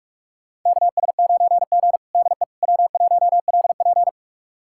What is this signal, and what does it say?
Morse code, heard as a keyed tone, about 42 wpm, 700 Hz. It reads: KS9G DE W1LP
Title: USA Callsign DE Callsign - 4 Characters - 45wpm